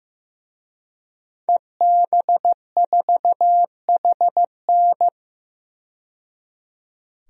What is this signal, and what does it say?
Morse code, heard as a keyed tone, about 16 wpm, 700 Hz. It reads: EB4HN
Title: DXCC-100 Call Signs - 5 Characters - 15wpm